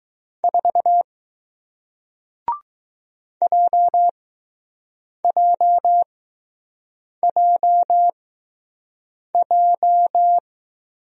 Morse code, heard as a keyed tone, about 20 wpm, spaced as for 6 wpm, 700 Hz. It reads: 4 JJJJ